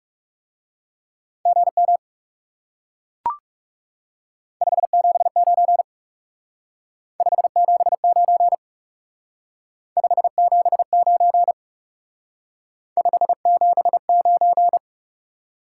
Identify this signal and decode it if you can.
Morse code, heard as a keyed tone, about 40 wpm, 700 Hz. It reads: GM 579 579 579 579